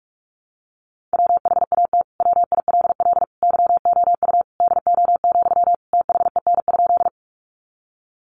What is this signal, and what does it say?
Morse code, heard as a keyed tone, about 44 wpm, 700 Hz. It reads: WHAT WILL YOU DO, THEN?
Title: Sentences from Top 100 Words 45wpm